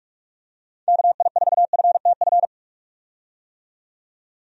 Morse code, heard as a keyed tone, about 46 wpm, 700 Hz. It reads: KI4FTF
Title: USA Call Signs - 6 Characters - 45wpm